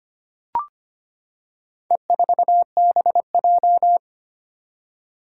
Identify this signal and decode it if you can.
Morse code, heard as a keyed tone, about 25 wpm, 700 Hz. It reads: E4BJ
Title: Four Challenging Letter-Number Combinations 25wpm